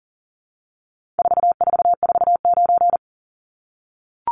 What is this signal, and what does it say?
Morse code, heard as a keyed tone, about 40 wpm, 700 Hz. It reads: 4449
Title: Numbers - 4-Digit 40wpm